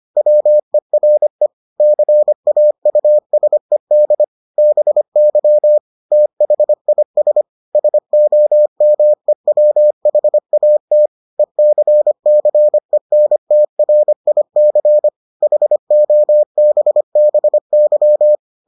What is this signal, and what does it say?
Morse code, heard as a keyed tone, about 25 wpm, 600 Hz. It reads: WERE CAUSED BY THIS SOMEWHAT ECCENTRIC HOBBY